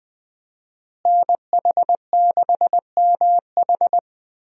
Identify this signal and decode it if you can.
Morse code, heard as a keyed tone, about 20 wpm, 700 Hz. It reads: NH6MH
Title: USA Call Signs - 5 Characters - Repeated 2x - 20wpm